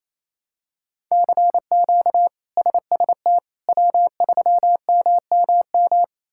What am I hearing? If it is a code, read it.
Morse code, heard as a keyed tone, about 28 wpm, 700 Hz. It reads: CQ SST W3MMM
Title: K1USN SST QSO Practice - 28wpm